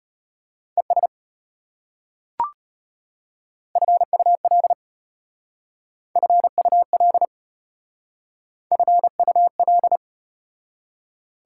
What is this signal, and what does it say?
Morse code, heard as a keyed tone, about 33 wpm, 700 Hz. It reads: ES FUL FUL FUL